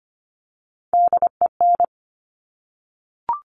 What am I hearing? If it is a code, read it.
Morse code, heard as a keyed tone, about 25 wpm, 700 Hz. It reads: DEN
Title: Top Three Letter Combinations in Words 25wpm